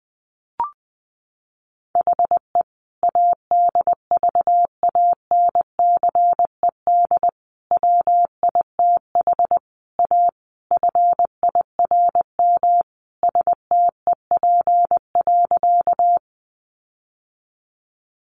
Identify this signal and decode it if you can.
Morse code, heard as a keyed tone, about 20 wpm, 700 Hz. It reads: HE ADVANCED WITH A FIRM STEP.
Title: Sentences from Top 1400 Words 20wpm